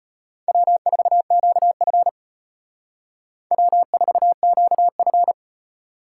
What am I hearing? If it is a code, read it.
Morse code, heard as a keyed tone, about 32 wpm, 700 Hz. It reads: W4QF W4QF